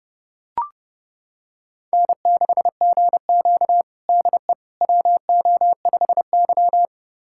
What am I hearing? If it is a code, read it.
Morse code, heard as a keyed tone, about 32 wpm, 700 Hz. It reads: N6GQ DE WO5Y
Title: USA Callsign DE Callsign - 4 Characters - 30wpm